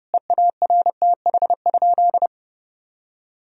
Morse code, heard as a keyed tone, about 30 wpm, 700 Hz. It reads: EARTH?
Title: Sentences from Top 1000 Words - Review Part 3 - 30wpm